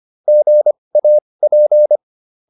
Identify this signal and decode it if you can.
Morse code, heard as a keyed tone, about 25 wpm, spaced as for 20 wpm, 600 Hz. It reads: GAP